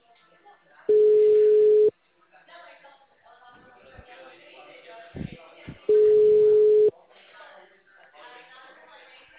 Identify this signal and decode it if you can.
Ringback tone (Europe, 425 Hz), with speech.
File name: out- -2011- - - .22452